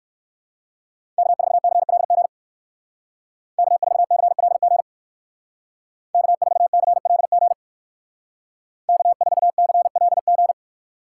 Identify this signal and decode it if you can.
Morse code, heard as a keyed tone, about 54 wpm, 700 Hz. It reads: K4CLG K4CLG K4CLG K4CLG
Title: Speed Racing - USA Call Signs 45wpm